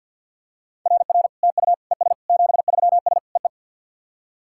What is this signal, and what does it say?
Morse code, heard as a keyed tone, about 52 wpm, 690 Hz. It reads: RR TU ES 73S EE
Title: SOTA QSO Practice 50wpm